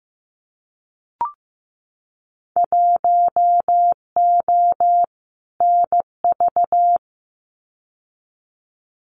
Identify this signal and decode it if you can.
Morse code, heard as a keyed tone, about 15 wpm, 700 Hz. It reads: 1O NV
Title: Winter Field Day Exchanges Encore 15wpm